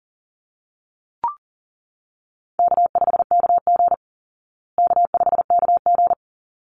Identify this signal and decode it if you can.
Morse code, heard as a keyed tone, about 40 wpm, 700 Hz. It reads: K5KG K5KG
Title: Easy Call Signs Repeated 2x - 40wpm